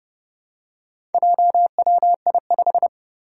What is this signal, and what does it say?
Morse code, heard as a keyed tone, about 30 wpm, 700 Hz. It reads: JWI5
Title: Four Challenging Letter-Number Combinations 30wpm